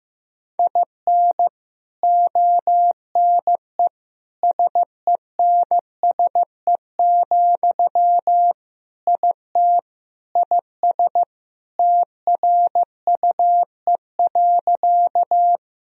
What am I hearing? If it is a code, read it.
Morse code, heard as a keyed tone, about 16 wpm, 700 Hz. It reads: IN ONE SENSE, IT IS TRUE.
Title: Sentences from Top 1000 Words - Review Part 4 - 15wpm